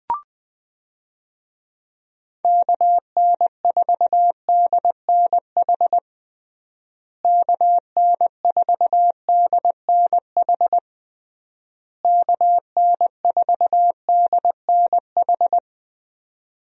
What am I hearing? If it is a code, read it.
Morse code, heard as a keyed tone, about 21 wpm, 700 Hz. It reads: KN4DNH KN4DNH KN4DNH
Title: USA Call Signs - 6 Characters - Repeated 3x - 20wpm